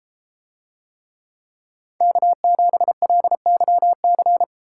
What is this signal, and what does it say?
Morse code, heard as a keyed tone, about 32 wpm, 700 Hz. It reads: K7LYC